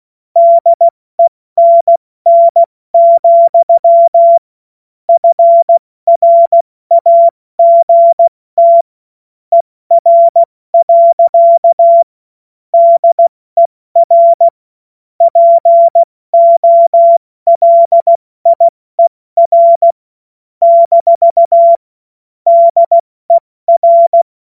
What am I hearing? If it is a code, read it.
Morse code, heard as a keyed tone, about 16 wpm, 680 Hz. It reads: DENN, FRAGT ER. DER POLIER - DER
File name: Morsekurs / Witze / Witz 20. Januar 2026 70 F80